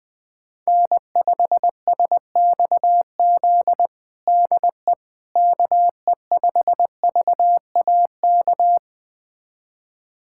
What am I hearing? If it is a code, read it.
Morse code, heard as a keyed tone, about 21 wpm, 700 Hz. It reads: N5SXZ DE KE5VAK
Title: USA Callsign DE Callsign 20wpm